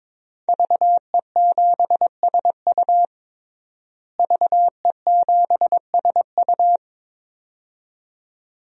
Morse code, heard as a keyed tone, about 22 wpm, 700 Hz. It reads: VE7SU VE7SU
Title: DXCC-100 Call Signs - 5 Characters - Repeated 2x - 22wpm